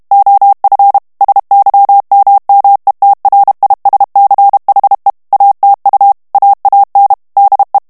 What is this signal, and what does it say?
Morse code, heard as a keyed tone, about 32 wpm, 800 Hz. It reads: OF SYMMETRISCHE ATU AAN DE
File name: morse mp3 / HOLLANDAIS texte clair 1 vitesse 30 mots minute